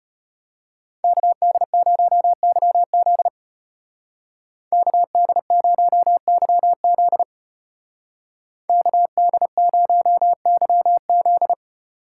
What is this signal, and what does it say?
Morse code, heard as a keyed tone, about 32 wpm, 700 Hz. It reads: KD0YZ KD0YZ KD0YZ